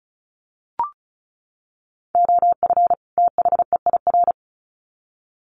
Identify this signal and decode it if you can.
Morse code, heard as a keyed tone, about 34 wpm, 700 Hz. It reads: OF THEIR
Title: Sets of 2 Words 35wpm